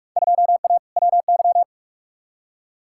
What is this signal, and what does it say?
Morse code, heard as a keyed tone, about 46 wpm, 700 Hz. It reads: 2A WY
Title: Field Day Exchanges - 45wpm